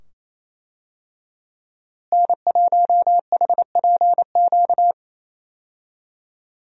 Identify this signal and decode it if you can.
Morse code, heard as a keyed tone, about 28 wpm, 700 Hz. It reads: N1HPQ
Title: USA Call Signs - 5 Characters - Repeated 2x - 28wpm